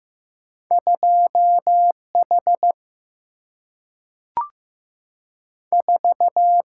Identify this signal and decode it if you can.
Morse code, heard as a keyed tone, about 15 wpm, 700 Hz. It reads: 2 H 4
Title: Single-Letter Single-Number 15wpm